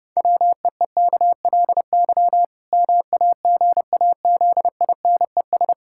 Morse code, heard as a keyed tone, about 30 wpm, 700 Hz. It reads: WEEKLY MAGAZINES
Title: Pangram Challenge v2 - 30wpm